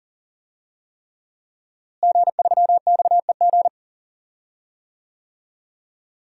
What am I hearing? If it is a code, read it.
Morse code, heard as a keyed tone, about 37 wpm, 700 Hz. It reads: G3XEG
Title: DXCC-100 Call Signs - 5 Characters - Repeated 3x - 40wpm